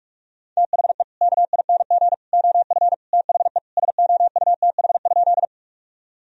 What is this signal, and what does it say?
Morse code, heard as a keyed tone, about 41 wpm, 700 Hz. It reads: THE KING OF THE SOUTH?
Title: Sentences from Top 1000 Words - Review Part 2 - 45wpm